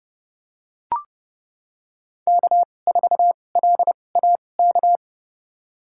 Morse code, heard as a keyed tone, about 30 wpm, spaced as for 22 wpm, 700 Hz. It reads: K4LAK
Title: USA Call Signs - 5 Characters - 2x Character Spacing - 30wpm